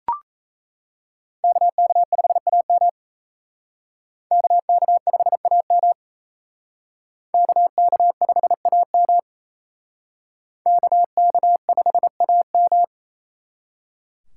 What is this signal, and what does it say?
Morse code, heard as a keyed tone, about 36 wpm, 700 Hz. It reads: KK5AM KK5AM KK5AM KK5AM